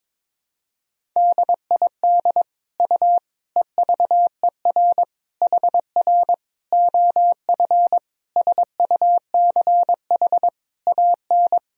Morse code, heard as a keyed tone, about 22 wpm, 700 Hz. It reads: DID U EVER HR OF SUCH AN
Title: Sentences from Top 400 Words and Abbreviations 22wpm